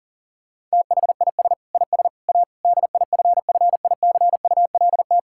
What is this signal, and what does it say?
Morse code, heard as a keyed tone, about 40 wpm, 700 Hz. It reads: THIS IS A DIFFICULT
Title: Sentences from Top 1000 Words - Review Part 5 - 40wpm